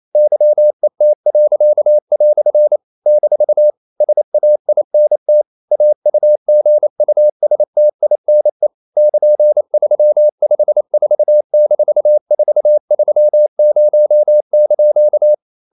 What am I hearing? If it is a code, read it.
Morse code, heard as a keyed tone, about 28 wpm, 600 Hz. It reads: YET." - SAINT AUGUSTINE <KN>354-430)
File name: CW / Practice / EnglishQuotations0128